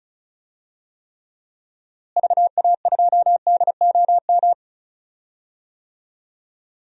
Morse code, heard as a keyed tone, about 36 wpm, 700 Hz. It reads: VA2DOM
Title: DXCC-100 Call Signs - 6 Characters - Repeated 3x - 35wpm